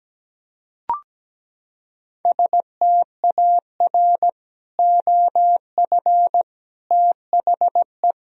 Morse code, heard as a keyed tone, about 17 wpm, 700 Hz. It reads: STAR OF THE
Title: Sentences from Top 1000 Words - Review Part 2 - 17wpm